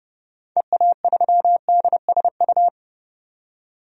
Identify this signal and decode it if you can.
Morse code, heard as a keyed tone, about 30 wpm, 700 Hz. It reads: EA3DSU
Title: DXCC-100 Call Signs - 6 Characters - 30wpm